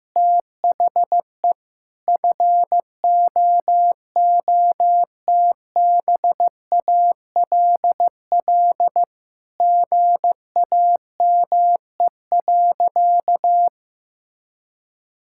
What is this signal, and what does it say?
Morse code, heard as a keyed tone, about 15 wpm, 700 Hz. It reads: THE FOOTBALL GAME.
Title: Sentences from Top 1600 Words 15wpm